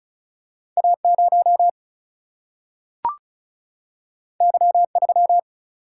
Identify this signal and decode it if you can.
Morse code, heard as a keyed tone, about 36 wpm, 700 Hz. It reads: A 0 Y 3
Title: Single-Letter Single-Number 35wpm